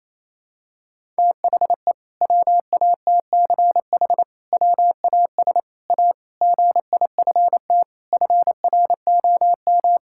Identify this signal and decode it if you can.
Morse code, heard as a keyed tone, about 27 wpm, 700 Hz. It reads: THE WATCH WAS A GIFT FROM
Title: Sentences from Top 1000 Words - Review Part 6 - 28wpm